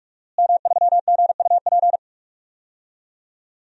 Morse code, heard as a keyed tone, about 46 wpm, 700 Hz. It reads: M3GUP
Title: DXCC-100 Call Signs - 5 Characters - Repeated 2x - 45wpm